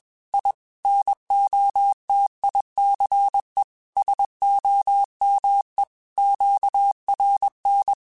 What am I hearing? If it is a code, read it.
Morse code, heard as a keyed tone, about 21 wpm, 800 Hz. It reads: I NOTICE SOME QRN
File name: QSOs / QSOs-mp3s / WD0BGP 1@20